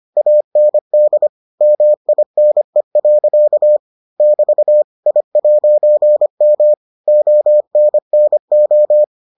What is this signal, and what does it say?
Morse code, heard as a keyed tone, about 25 wpm, 600 Hz. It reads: AND MINE. <BT> I'M ONNO